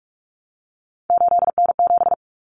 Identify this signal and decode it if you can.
Morse code, heard as a keyed tone, about 44 wpm, 700 Hz. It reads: 8N7
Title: Common Cut Numbers - 3-Digit 45wpm